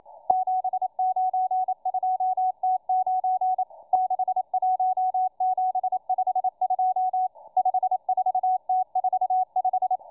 Morse code, heard as a keyed tone, about 27 wpm, 740 Hz, with radio noise. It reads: 792T9 61752 54T45